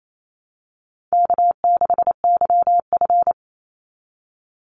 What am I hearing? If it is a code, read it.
Morse code, heard as a keyed tone, about 28 wpm, 700 Hz. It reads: K6YF
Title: Easy Call Signs Repeated 3x - 28wpm